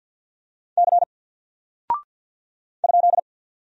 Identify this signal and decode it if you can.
Morse code, heard as a keyed tone, about 48 wpm, 700 Hz. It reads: C ?
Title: Single Character (Letter:Number:Punctuation:Prosign) 50wpm